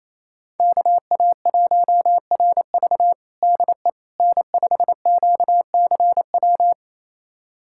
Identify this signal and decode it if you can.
Morse code, heard as a keyed tone, about 28 wpm, 700 Hz. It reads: KA1RV DE N5QCW
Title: USA Callsign DE Callsign - 5 Characters - 28wpm